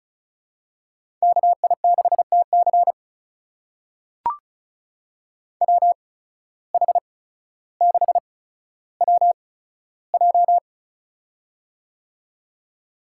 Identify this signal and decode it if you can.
Morse code, heard as a keyed tone, about 36 wpm, spaced as for 15 wpm, 700 Hz. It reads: KI6TC WH6WJ
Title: USA Call Signs - 5 Characters - 8x Character Spacing - 35wpm